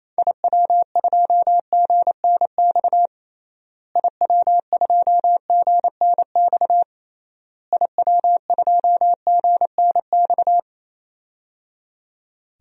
Morse code, heard as a keyed tone, about 27 wpm, 700 Hz. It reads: IW2GNX IW2GNX IW2GNX
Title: DXCC-100 Call Signs - 6 Characters - Repeated 3x - 28wpm